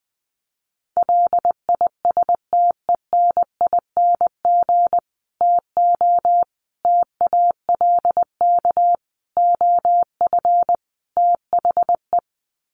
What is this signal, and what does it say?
Morse code, heard as a keyed tone, about 20 wpm, 700 Hz. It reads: LISTENING TO TALK OF THE
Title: Sets of 5 Words 20wpm